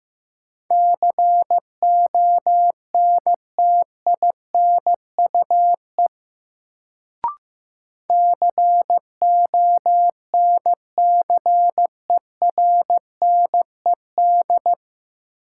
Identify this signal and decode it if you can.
Morse code, heard as a keyed tone, about 15 wpm, 700 Hz. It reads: CONTINUE CONCERNED
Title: Sentences from Top 1000 Words - Review Part 6 - New Word Drill - 15wpm